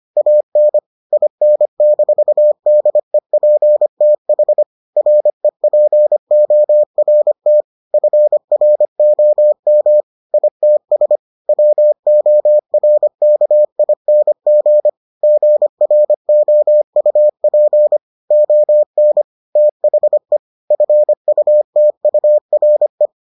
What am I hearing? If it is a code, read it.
Morse code, heard as a keyed tone, about 25 wpm, 600 Hz. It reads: AN IN-DEPTH REPORT FROM ITS WORKING GROUP ON THE FUTURE